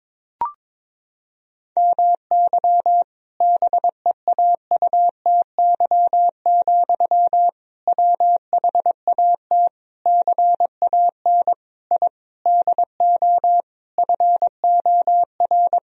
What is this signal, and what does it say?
Morse code, heard as a keyed tone, about 22 wpm, 700 Hz. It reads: MY BEAUTY, WHAT CAN I DO FOR
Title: Sentences from Top 1000 Words - Review Part 2 - 22wpm